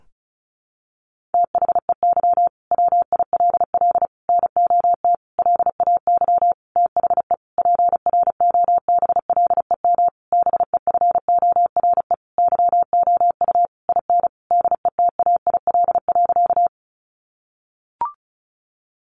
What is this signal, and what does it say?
Morse code, heard as a keyed tone, about 34 wpm, 700 Hz. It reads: THEY WILL NOT LAY THE PROBLEM BEFORE YOU IN DETAIL.
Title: Sentences from Top 1500 Words 35wpm